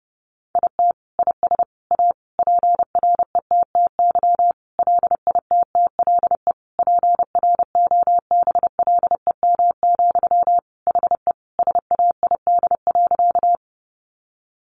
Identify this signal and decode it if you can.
Morse code, heard as a keyed tone, about 30 wpm, 700 Hz. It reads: IT IS A PRETTY LITTLE PROBLEM, HE SAID.
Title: Sentences from Top 1000 Words 30wpm